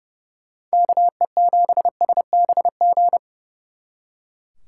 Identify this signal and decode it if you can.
Morse code, heard as a keyed tone, about 36 wpm, 700 Hz. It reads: KE7SBG
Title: Speed Racing - USA 2x3 Call Signs 30wpm